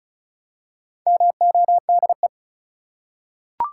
Morse code, heard as a keyed tone, about 35 wpm, 700 Hz. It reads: MODE
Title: Four Letter Words - 35wpm